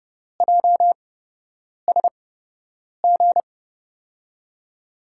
Morse code, heard as a keyed tone, about 30 wpm, spaced as for 8 wpm, 700 Hz. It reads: JSG
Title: USA Call Signs - 8x Character Spacing - 30wpm